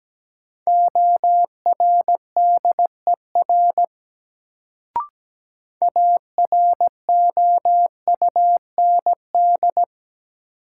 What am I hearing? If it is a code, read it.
Morse code, heard as a keyed tone, about 17 wpm, 700 Hz. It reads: ORDER AROUND
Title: Sets of 2 Words - New Word Drill - 17wpm